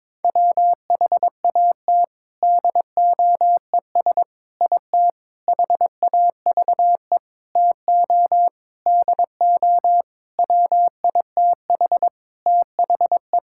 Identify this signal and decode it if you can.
Morse code, heard as a keyed tone, about 22 wpm, 700 Hz. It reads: WHAT DOES IT HAVE TO DO WITH THE
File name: practice / 22wpm / Sentences with 4-Word Phrases - 22wpm